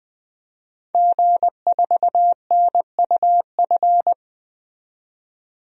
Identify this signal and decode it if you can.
Morse code, heard as a keyed tone, about 20 wpm, 700 Hz. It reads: G4NUF
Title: DXCC-100 Call Signs - 5 Characters - 20wpm